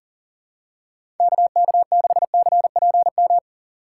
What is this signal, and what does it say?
Morse code, heard as a keyed tone, about 40 wpm, 700 Hz. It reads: KK6CPM
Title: USA Call Signs - 6 Characters - 3x Character Spacing - 40wpm